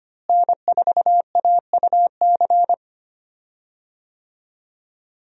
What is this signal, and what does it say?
Morse code, heard as a keyed tone, about 25 wpm, 700 Hz. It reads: N4AUC
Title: USA Call Signs - 5 Characters - Repeated 2x - 25wpm